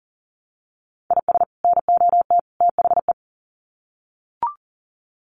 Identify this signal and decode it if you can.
Morse code, heard as a keyed tone, about 40 wpm, 700 Hz. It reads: IS NOT THE
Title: Sets of 3 Words 40wpm